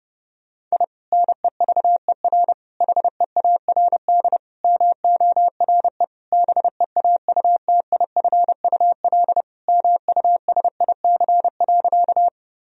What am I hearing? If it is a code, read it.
Morse code, heard as a keyed tone, about 30 wpm, 700 Hz. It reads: I NEVER HEARD MORE BEAUTIFUL MUSIC.
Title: Sentences from Top 1000 Words - Review Part 1 - 30wpm